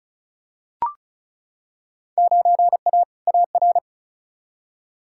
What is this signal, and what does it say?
Morse code, heard as a keyed tone, about 34 wpm, 700 Hz. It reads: 9A AR
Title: Field Day Exchanges Encore - 35wpm